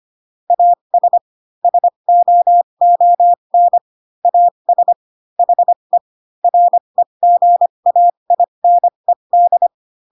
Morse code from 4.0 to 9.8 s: AS HE REGAINED